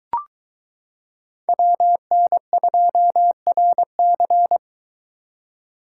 Morse code, heard as a keyed tone, about 23 wpm, 700 Hz. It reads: WN2RC